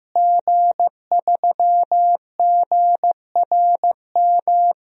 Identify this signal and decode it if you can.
Morse code, heard as a keyed tone, about 15 wpm, 700 Hz. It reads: G3GRM
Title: DXCC-100 Call Signs - 5 Characters - Repeated 3x - 15wpm